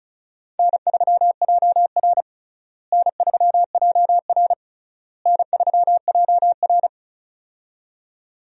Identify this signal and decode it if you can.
Morse code, heard as a keyed tone, about 36 wpm, 700 Hz. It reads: N3JR N3JR N3JR
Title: USA Call Signs - 4 Characters - Repeated 3x - 35wpm